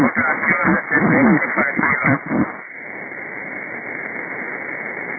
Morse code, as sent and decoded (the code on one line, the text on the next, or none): none